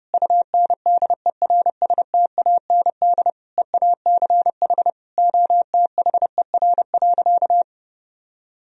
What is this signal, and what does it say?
Morse code, heard as a keyed tone, about 32 wpm, 700 Hz. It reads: UNDERSTAND EACH OTHER.
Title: Sentences from Top 1000 Words - Review Part 1 - 30wpm